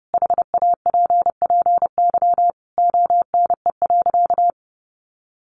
Morse code, heard as a keyed tone, about 30 wpm, 700 Hz. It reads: HAPPY ONE.
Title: Sentences from Top 1400 Words 30wpm